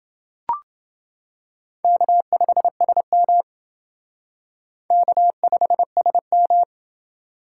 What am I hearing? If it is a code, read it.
Morse code, heard as a keyed tone, about 29 wpm, 700 Hz. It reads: K5SM K5SM